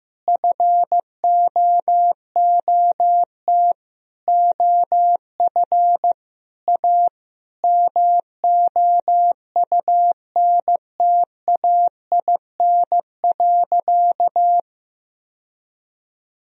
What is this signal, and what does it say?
Morse code, heard as a keyed tone, about 15 wpm, 700 Hz. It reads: FOOT OF A MOUNTAIN.
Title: Sentences from Top 1000 Words - Review Part 3 - 15wpm